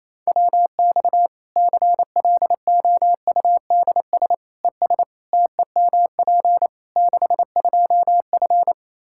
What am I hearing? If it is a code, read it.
Morse code, heard as a keyed tone, about 28 wpm, 700 Hz. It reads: WX CLOUDS ES TEMP 62F
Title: K1USN SST QSO Practice 28wpm